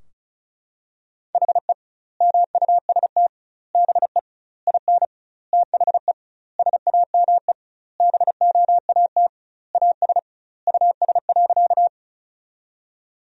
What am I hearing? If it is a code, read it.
Morse code, heard as a keyed tone, about 36 wpm, 700 Hz. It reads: HE MUST BE IN THE SAME BOAT AS US.
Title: 2x Word-Spacing - Sentences from Top 500 Words - 35wpm